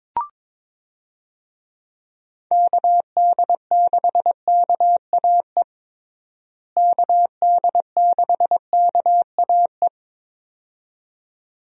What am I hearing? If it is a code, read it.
Morse code, heard as a keyed tone, about 22 wpm, 700 Hz. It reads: KD6KAE KD6KAE
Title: USA Call Signs - 6 Characters - Repeated 2x - 22wpm